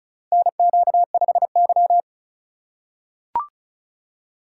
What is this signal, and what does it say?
Morse code, heard as a keyed tone, about 35 wpm, 700 Hz. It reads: NQ5Y
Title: USA Call Signs - 6x Character Spacing - 35wpm